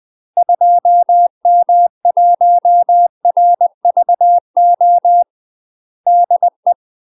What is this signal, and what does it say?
Morse code, heard as a keyed tone, about 20 wpm, 700 Hz. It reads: 2M1RVO DE
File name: UK MASTER QSOs TRAINING / 20WPM / UK MASTER QSOs 0053